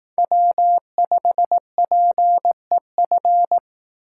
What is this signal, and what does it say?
Morse code, heard as a keyed tone, about 18 wpm, 700 Hz. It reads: W5PEF
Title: Speed Racing - USA Call Signs 15wpm